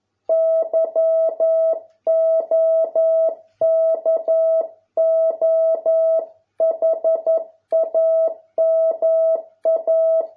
Morse code, heard as a keyed tone, about 11 wpm, 650 Hz. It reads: YOKOHAMA